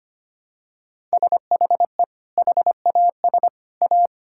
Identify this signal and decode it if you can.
Morse code, heard as a keyed tone, about 25 wpm, 700 Hz. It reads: SHE HAS A
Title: Sentences from Top 1000 Words - Review Part 5 - 25wpm